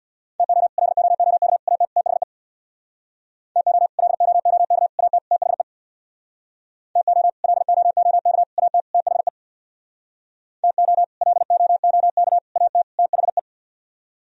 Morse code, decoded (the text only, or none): TO LOOK AT THE TO LOOK AT THE TO LOOK AT THE TO LOOK AT THE